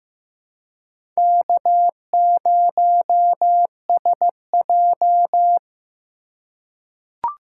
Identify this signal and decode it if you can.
Morse code, heard as a keyed tone, about 15 wpm, 700 Hz. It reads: K0SJ
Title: USA Call Signs - 4 Characters - 2x Character Spacing - 15wpm